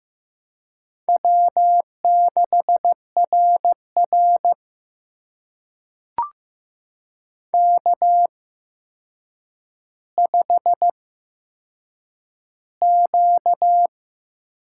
Morse code, heard as a keyed tone, about 15 wpm, spaced as for 7 wpm, 700 Hz. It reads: W6RR K5Q
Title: USA Call Signs - 4 Characters - 8x Character Spacing - 15wpm